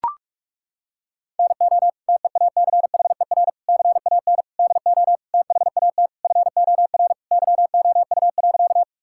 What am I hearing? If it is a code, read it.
Morse code, heard as a keyed tone, about 41 wpm, 700 Hz. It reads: NO TEACHER CAN DO THAT FOR YOU.
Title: Sentences from Top 1000 Words - Review Part 6 - 45wpm